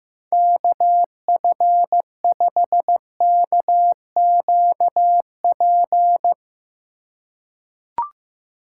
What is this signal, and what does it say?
Morse code, heard as a keyed tone, about 16 wpm, 700 Hz. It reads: KF5KQP
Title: USA Call Signs - 6 Characters - 2x Character Spacing - 15wpm